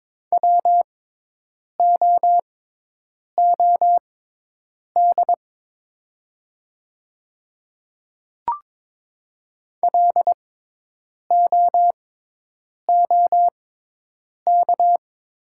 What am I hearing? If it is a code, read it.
Morse code, heard as a keyed tone, about 22 wpm, spaced as for 8 wpm, 690 Hz. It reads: WOOD LOOK
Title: 6x Character Spacing - Top 300 Words - 22wpm